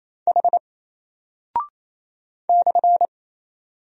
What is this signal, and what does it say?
Morse code, heard as a keyed tone, about 28 wpm, 700 Hz. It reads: H /
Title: Single Character (Letter:Number:Punctuation:Prosign) Alt 28wpm